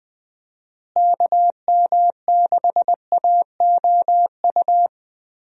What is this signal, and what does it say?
Morse code, heard as a keyed tone, about 20 wpm, 700 Hz. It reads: KM6AOU
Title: USA Call Signs - 6 Characters - Repeated 3x - 20wpm